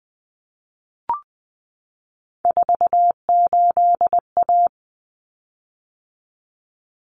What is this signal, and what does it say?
Morse code, heard as a keyed tone, about 20 wpm, 700 Hz. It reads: 48A
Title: Common Cut Numbers - 3-Digit 20wpm